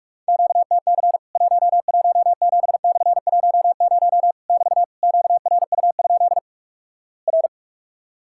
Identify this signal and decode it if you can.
Morse code, heard as a keyed tone, about 44 wpm, 700 Hz. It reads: QTC 117/10 <BT> QRU? R
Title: Worked All Europe DX Contest - QTC Exchange - 45wpm